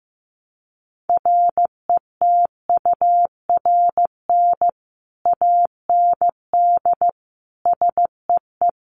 Morse code, heard as a keyed tone, about 14 wpm, 700 Hz. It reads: RETURN AND SEE
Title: Sentences from Top 800 Words 15wpm